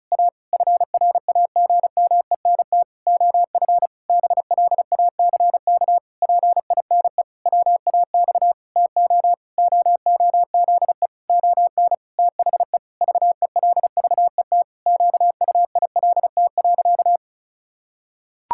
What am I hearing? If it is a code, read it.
Morse code, heard as a keyed tone, about 34 wpm, 700 Hz. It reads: A FRAGMENT OF BLACK PINE WAX TO OOZE ON THE VELVET QUILT.
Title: Pangram Challenge v2 - 35wpm